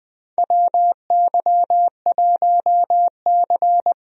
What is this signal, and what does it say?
Morse code, heard as a keyed tone, about 20 wpm, 700 Hz. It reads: WY1C
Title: USA Call Signs - 4 Characters - Repeated 3x - 20wpm